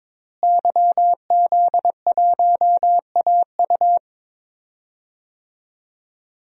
Morse code, heard as a keyed tone, about 22 wpm, 700 Hz. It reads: YZ1AU
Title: DXCC-100 Call Signs - 5 Characters - Repeated 2x - 22wpm